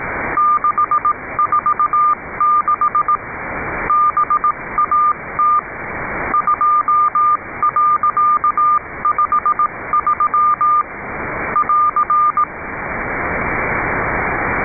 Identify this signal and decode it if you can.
Morse code, heard as a keyed tone, about 18 wpm, 1200 Hz. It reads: 646 BAT 2.53 <AR>
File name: Icaro5 CW teste